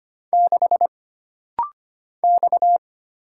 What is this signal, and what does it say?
Morse code, heard as a keyed tone, about 25 wpm, 700 Hz. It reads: 6 X
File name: Single Character (Letter:Number:Punctuation:Prosign) - Rapid-Fire - 25wpm